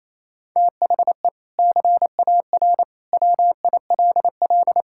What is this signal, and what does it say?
Morse code, heard as a keyed tone, about 28 wpm, 700 Hz. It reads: THE CAR WILL
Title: Sentences from Top 1600 Words 28wpm